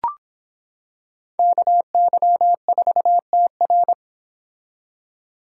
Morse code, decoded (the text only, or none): KY4TR